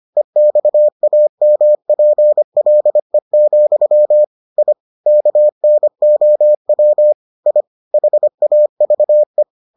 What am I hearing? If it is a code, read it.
Morse code, heard as a keyed tone, about 25 wpm, 600 Hz. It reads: EXAMPLE, I KNOW I HAVE